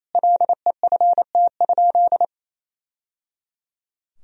Morse code, heard as a keyed tone, about 28 wpm, 700 Hz. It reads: LEFT?
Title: Sentences from Top 1000 Words - Review Part 2 - 28wpm